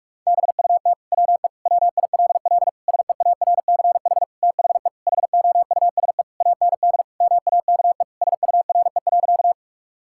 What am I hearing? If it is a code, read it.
Morse code, heard as a keyed tone, about 46 wpm, 700 Hz. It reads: BUT WE WILL SEARCH THE HOUSE AND MAKE SURE.
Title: Sentences from Top 1000 Words - Review Part 5 - 45wpm